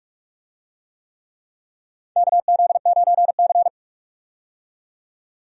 Morse code, heard as a keyed tone, about 46 wpm, 700 Hz. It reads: KZ9C
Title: USA Call Signs - 4 Characters - 45wpm